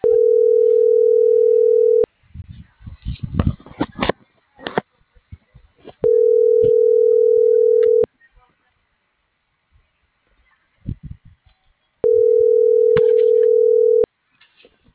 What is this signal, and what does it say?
Ringback tone (North America), 440 + 480 Hz.